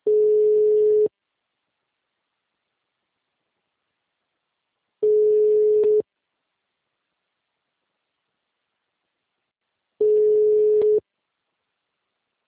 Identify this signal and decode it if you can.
Ringback tone (Europe, 425 Hz), with speech.
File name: out- -2021- - - .28739